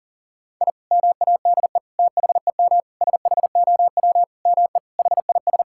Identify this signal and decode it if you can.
Morse code, heard as a keyed tone, about 37 wpm, 700 Hz. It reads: I MADE THEM SHOW ME HIS
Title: Sentences from Top 1000 Words - Review Part 6 - 40wpm